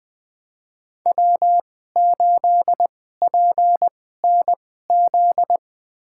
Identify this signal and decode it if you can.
Morse code, heard as a keyed tone, about 21 wpm, spaced as for 15 wpm, 700 Hz. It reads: W8PNZ
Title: USA Call Signs - 5 Characters - 2x Character Spacing - 20wpm